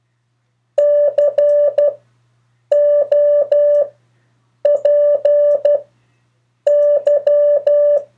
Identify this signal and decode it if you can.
Morse code, heard as a keyed tone, about 12 wpm, spaced as for 7 wpm, 580 Hz. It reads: COPY
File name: 3of4-words-2